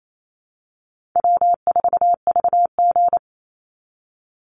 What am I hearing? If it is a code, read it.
Morse code, heard as a keyed tone, about 28 wpm, 700 Hz. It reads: W4VG
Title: Easy Call Signs Repeated 2x - 28wpm